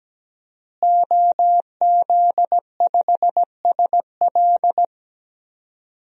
Morse code, heard as a keyed tone, about 17 wpm, 700 Hz. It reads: OZ5SL
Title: DXCC-100 Call Signs - 5 Characters - Repeated 2x - 17wpm